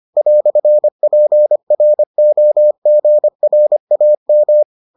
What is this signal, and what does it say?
Morse code, heard as a keyed tone, about 25 wpm, 600 Hz. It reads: "PROGRAM